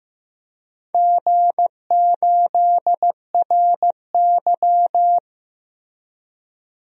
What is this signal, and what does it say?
Morse code, heard as a keyed tone, about 15 wpm, 700 Hz. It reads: G8RY
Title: DXCC-100 Call Signs - 4 Characters - Repeated 3x - 15wpm